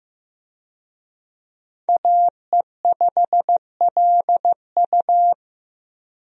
Morse code, heard as a keyed tone, about 15 wpm, 700 Hz. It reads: AE5LU
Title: USA Call Signs - 5 Characters - Repeated 3x - 15wpm